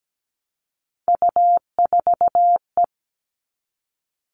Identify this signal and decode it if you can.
Morse code, heard as a keyed tone, about 17 wpm, 700 Hz. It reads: U4E
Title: All Cut Numbers - 3-Digit 17wpm